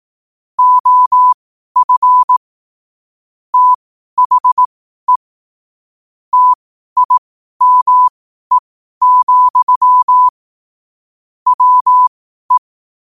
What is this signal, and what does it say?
Morse code, heard as a keyed tone, about 18 wpm, spaced as for 12 wpm, 1000 Hz. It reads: OF THE TIME, WE
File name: qotdfiles / 12wpm